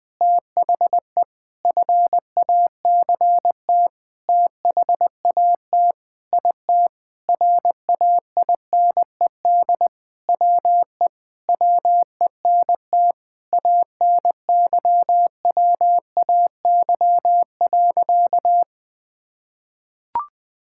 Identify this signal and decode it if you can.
Morse code, heard as a keyed tone, about 20 wpm, 700 Hz. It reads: THE FACT THAT IT RAINED WE WENT ANYWAY.
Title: Sentences with 4-Word Phrases - Spoken Beginning - 20wpm